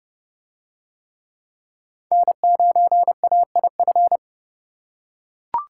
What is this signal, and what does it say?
Morse code, heard as a keyed tone, about 28 wpm, 700 Hz. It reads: N9AIF